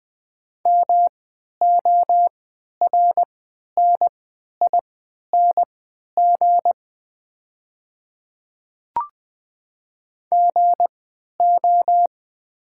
Morse code, heard as a keyed tone, about 20 wpm, spaced as for 11 wpm, 700 Hz. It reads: MORNING GO
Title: 3x Character Spacing - Top 500 Words - 20wpm